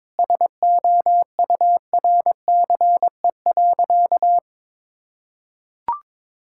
Morse code, heard as a keyed tone, about 22 wpm, 700 Hz. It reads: SOURCE.
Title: Sentences from Top 1000 Words - Review Part 5 - 22wpm